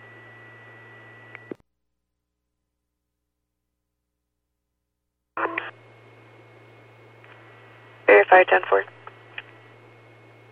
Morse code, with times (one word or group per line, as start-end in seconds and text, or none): none